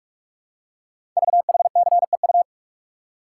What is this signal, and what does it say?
Morse code, heard as a keyed tone, about 44 wpm, 690 Hz. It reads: VHCEU
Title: Code Groups - 5 Letters - 45wpm